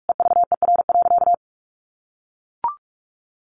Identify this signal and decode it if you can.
Morse code, heard as a keyed tone, about 44 wpm, 700 Hz. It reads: EVER.
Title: Sentences from Top 1300 Words 45wpm